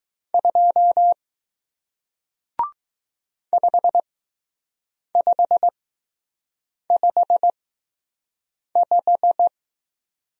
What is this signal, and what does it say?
Morse code, heard as a keyed tone, about 20 wpm, spaced as for 6 wpm, 700 Hz. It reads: 2 5555